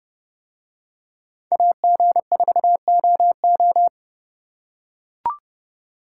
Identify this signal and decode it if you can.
Morse code, heard as a keyed tone, about 31 wpm, 700 Hz. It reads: AG4OO